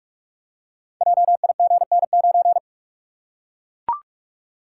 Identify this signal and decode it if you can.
Morse code, heard as a keyed tone, about 44 wpm, 700 Hz. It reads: JIGN9
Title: Code Groups - 5 Letter:Numbers - 45wpm